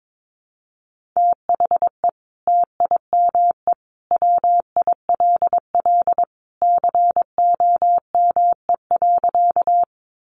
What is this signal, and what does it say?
Morse code, heard as a keyed tone, about 22 wpm, 700 Hz. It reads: THE TIME WILL COME.
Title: Sentences from Top 100 Words 22wpm